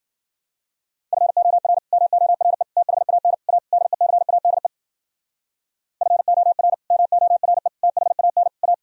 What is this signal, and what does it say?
Morse code, heard as a keyed tone, about 53 wpm, 700 Hz. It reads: FOR MORE THAN A DECADE FOR MORE THAN A